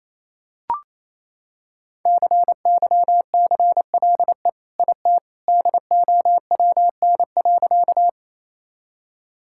Morse code, heard as a keyed tone, about 28 wpm, 700 Hz. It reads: CYCLE IT DOWN.
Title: Sentences from Top 1000 Words - Review Part 6 - 28wpm